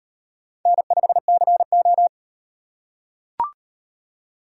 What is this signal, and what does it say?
Morse code, heard as a keyed tone, about 38 wpm, 700 Hz. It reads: N5CO